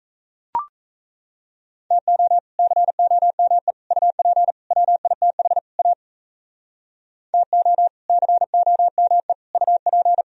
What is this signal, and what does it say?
Morse code, heard as a keyed tone, about 38 wpm, 700 Hz. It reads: TO COME UP WITH A TO COME UP